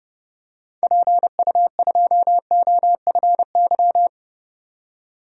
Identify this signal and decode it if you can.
Morse code, heard as a keyed tone, about 30 wpm, 700 Hz. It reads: PU2OFY
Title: DXCC-100 Call Signs - 6 Characters - 30wpm